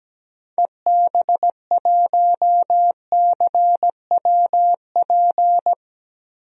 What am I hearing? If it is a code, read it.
Morse code, heard as a keyed tone, about 17 wpm, 700 Hz. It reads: EB1CWP
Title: DXCC-100 Call Signs - 6 Characters - Repeated 3x - 17wpm